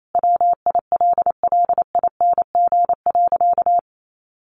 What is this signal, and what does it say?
Morse code, heard as a keyed tone, about 28 wpm, 700 Hz. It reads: WILLING.
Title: Sentences from Top 1400 Words 28wpm